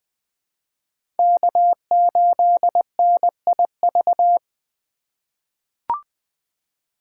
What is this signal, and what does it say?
Morse code, heard as a keyed tone, about 20 wpm, 700 Hz. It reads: K8NIV
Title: 2023 POTA Hunter Call Signs - 20wpm